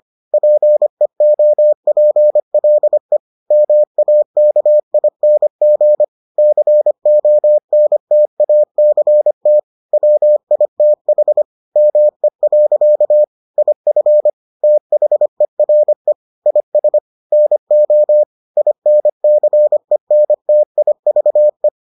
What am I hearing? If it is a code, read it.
Morse code, heard as a keyed tone, about 25 wpm, 600 Hz. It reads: PEOPLE MAKING CONTACT WITH ME. IF THERE IS NO INCENTIVE